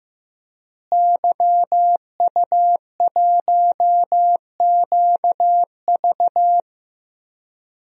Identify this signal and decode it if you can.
Morse code, heard as a keyed tone, about 15 wpm, 700 Hz. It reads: YU1QV
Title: DXCC-100 Call Signs - 5 Characters - Repeated 2x - 15wpm